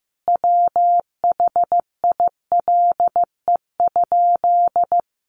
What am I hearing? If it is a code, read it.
Morse code, heard as a keyed tone, about 15 wpm, 700 Hz. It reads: WHILE?
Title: Sentences from Top 300 Words 15wpm